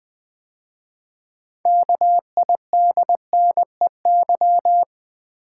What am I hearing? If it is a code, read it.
Morse code, heard as a keyed tone, about 20 wpm, 700 Hz. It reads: KIDNEY
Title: 6 Letter Words - First 3 Letters Given - 20wpm